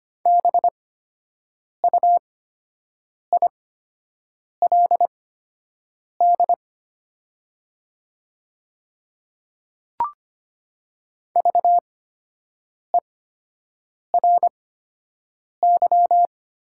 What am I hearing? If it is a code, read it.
Morse code, heard as a keyed tone, about 25 wpm, spaced as for 7 wpm, 700 Hz. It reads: BUILD VERY
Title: 8x Character Spacing - Top 200 Words - 25wpm